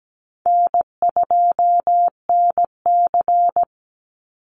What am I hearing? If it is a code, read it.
Morse code, heard as a keyed tone, about 17 wpm, 700 Hz. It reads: N2NC
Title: Easy Call Signs - 17wpm